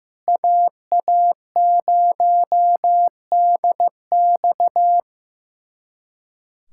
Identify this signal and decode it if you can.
Morse code, heard as a keyed tone, about 15 wpm, 700 Hz. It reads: AA0DX
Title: USA Callsign DE Callsign 15wpm